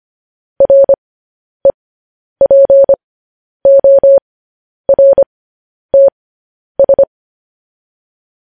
Morse code, heard as a keyed tone, about 25 wpm, spaced as for 10 wpm, 550 Hz. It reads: REPORTS
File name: podcast / News headlines-1x-25x10